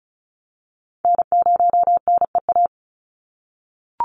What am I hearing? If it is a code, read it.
Morse code, heard as a keyed tone, about 34 wpm, 700 Hz. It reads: N0NEA
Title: USA Call Signs 35wpm